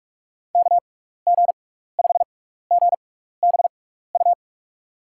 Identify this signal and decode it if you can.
Morse code, heard as a keyed tone, about 46 wpm, spaced as for 15 wpm, 700 Hz. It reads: KG5GBU
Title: USA Call Signs - 6 Characters - 6x Character Spacing - 45wpm